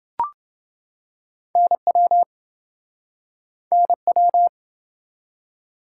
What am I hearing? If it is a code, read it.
Morse code, heard as a keyed tone, about 29 wpm, 700 Hz. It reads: NW NW